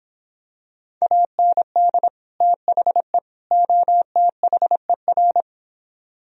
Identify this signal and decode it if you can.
Morse code, heard as a keyed tone, about 26 wpm, 700 Hz. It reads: AND THE OTHER
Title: Speed-Racing - Sets of 3 Words 22wpm